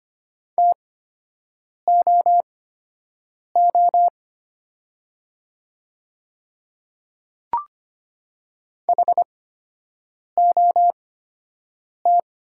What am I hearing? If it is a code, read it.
Morse code, heard as a keyed tone, about 25 wpm, spaced as for 7 wpm, 700 Hz. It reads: TOO HOT
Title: 8x Character Spacing - Top 500 Words - 25wpm